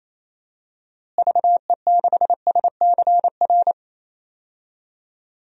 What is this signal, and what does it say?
Morse code, heard as a keyed tone, about 27 wpm, 700 Hz. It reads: VE6SCR
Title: DXCC-100 Call Signs - 6 Characters - Repeated 3x - 28wpm